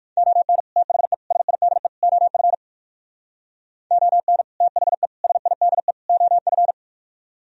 Morse code, heard as a keyed tone, about 65 wpm, spaced as for 56 wpm, 700 Hz. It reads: ON THE SIDE OF ON THE SIDE OF